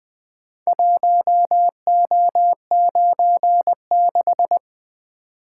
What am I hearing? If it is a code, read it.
Morse code, heard as a keyed tone, about 20 wpm, 700 Hz. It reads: 1O96
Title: Four Letter-Number Combinations - 20wpm